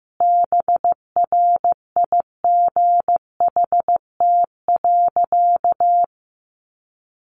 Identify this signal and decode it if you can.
Morse code, heard as a keyed tone, about 15 wpm, 700 Hz. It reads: BRIGHT.
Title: Sentences from Top 1500 Words 15wpm